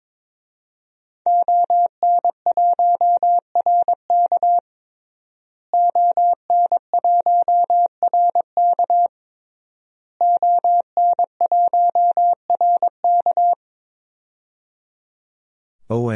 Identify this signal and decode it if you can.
Morse code, heard as a keyed tone, about 21 wpm, 700 Hz. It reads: ON1RK ON1RK ON1RK
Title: DXCC-100 Call Signs - 5 Characters - Repeated 3x - 22wpm